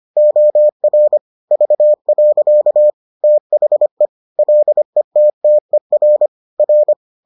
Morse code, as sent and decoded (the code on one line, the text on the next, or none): --- .-. / ...- .-.-.- / - .... / .-.. - - . .-. / .-.
OR V. THE LETTER R